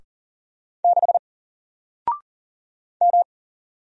Morse code, heard as a keyed tone, about 40 wpm, 700 Hz. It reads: 6 M
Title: Single Character (Letter:Number:Punctuation:Prosign) Alt 40wpm